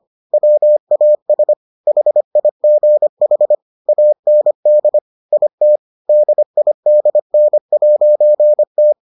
Morse code, heard as a keyed tone, about 25 wpm, 600 Hz. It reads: WAS HIGH AND IT DIDN'T